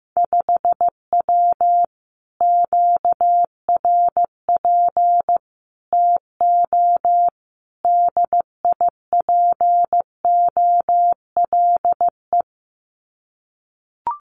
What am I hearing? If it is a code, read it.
Morse code, heard as a keyed tone, about 15 wpm, 700 Hz. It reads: 5W QRP TO DIPOLE
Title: Common QSO Phrases Encore 15wpm